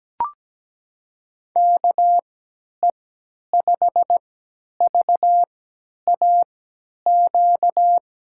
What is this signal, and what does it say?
Morse code, heard as a keyed tone, about 17 wpm, spaced as for 10 wpm, 700 Hz. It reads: KE5VAQ
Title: USA Call Signs - 6 Characters - 3x Character Spacing - 17wpm